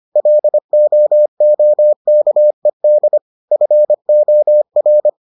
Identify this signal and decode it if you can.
Morse code, heard as a keyed tone, about 25 wpm, 600 Hz. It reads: LOOKED FOR